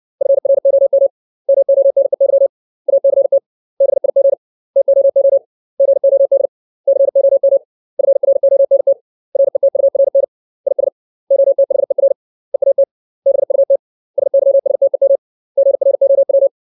Morse code, heard as a keyed tone, about 60 wpm, 550 Hz. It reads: FROM MONEY ACT BIG TOO GOD COM FRONT RETURN IS OTHER EAT BUT SYSTEM GROW
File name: 632 words / 632 Words 60WPM